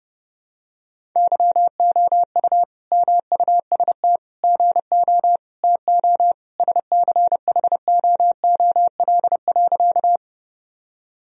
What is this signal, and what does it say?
Morse code, heard as a keyed tone, about 32 wpm, 700 Hz. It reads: YOU MUST GO TO SCHOOL.
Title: Sentences from Top 1000 Words - Review Part 1 - 30wpm